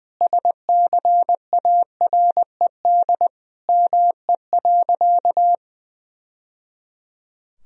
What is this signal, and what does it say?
Morse code, heard as a keyed tone, about 20 wpm, 700 Hz. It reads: SCARED ME.
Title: Sentences from Top 1000 Words - Review Part 4 - 20wpm